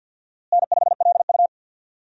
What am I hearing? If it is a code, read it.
Morse code, heard as a keyed tone, about 48 wpm, 700 Hz. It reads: N5LU
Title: USA Call Signs - 4 Characters - Repeated 3x - 50wpm